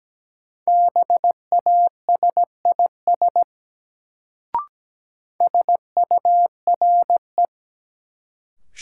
Morse code, heard as a keyed tone, about 17 wpm, 700 Hz. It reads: BASIS SURE
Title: Sets of 3 Words - New Word Drill - 17wpm